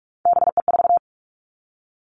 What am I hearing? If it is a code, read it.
Morse code, heard as a keyed tone, about 46 wpm, 700 Hz. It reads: BE4
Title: Three Challenging Letter-Number Combinations 45wpm